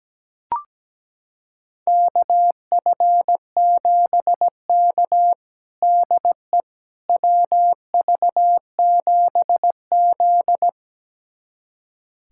Morse code, heard as a keyed tone, about 17 wpm, 700 Hz. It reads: KF7K DE WV7Z
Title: USA Callsign DE Callsign - 4 Characters - 17wpm